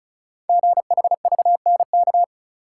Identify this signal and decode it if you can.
Morse code, heard as a keyed tone, about 36 wpm, 690 Hz. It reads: GHVNK
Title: Code Groups - 5 Letters - 35wpm